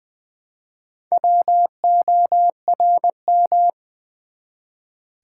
Morse code, heard as a keyed tone, about 20 wpm, 700 Hz. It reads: WORM
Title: Build A Word - 1 Letter At A Time - 4-Letter Words - 20wpm